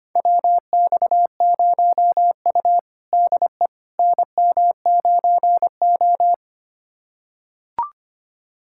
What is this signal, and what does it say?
Morse code, heard as a keyed tone, about 25 wpm, 700 Hz. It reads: WX0U DE NM9O
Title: USA Callsign DE Callsign - 4 Characters - 25wpm